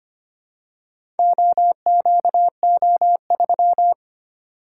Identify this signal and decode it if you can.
Morse code, heard as a keyed tone, about 25 wpm, 700 Hz. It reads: OQO3
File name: practice / 25wpm / Four Letter-Number Combinations - 25wpm